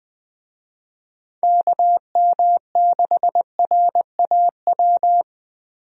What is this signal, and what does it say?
Morse code, heard as a keyed tone, about 20 wpm, 700 Hz. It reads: KM6RAW
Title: USA Call Signs - 6 Characters - Repeated 3x - 20wpm